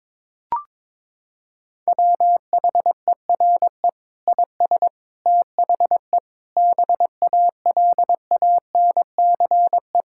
Morse code, heard as a keyed tone, about 22 wpm, 700 Hz. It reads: WHERE IS THE BALANCE
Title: Sentences from Top 1000 Words - Review Part 6 - 22wpm